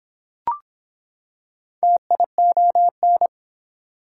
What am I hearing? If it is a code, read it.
Morse code, heard as a keyed tone, about 25 wpm, 700 Hz. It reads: TION